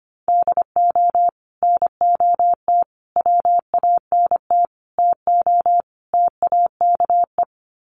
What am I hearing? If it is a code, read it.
Morse code, heard as a keyed tone, about 25 wpm, 700 Hz. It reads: DO NOT WANT TO TAKE
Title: Sets of 5 Words Encore2 25wpm